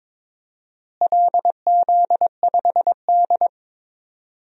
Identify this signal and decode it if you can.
Morse code, heard as a keyed tone, about 22 wpm, 700 Hz. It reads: LZ5D
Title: DXCC-100 Call Signs - 4 Characters - Repeated 2x - 22wpm